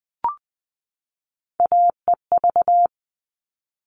Morse code, heard as a keyed tone, about 20 wpm, 700 Hz. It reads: AEV
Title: All Cut Numbers - 3-Digit 20wpm